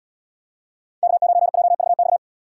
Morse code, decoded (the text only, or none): K9JVX